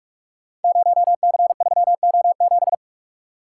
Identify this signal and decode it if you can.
Morse code, heard as a keyed tone, about 44 wpm, 690 Hz. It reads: 0C3O7
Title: Code Groups - 5 Letter:Numbers - 45wpm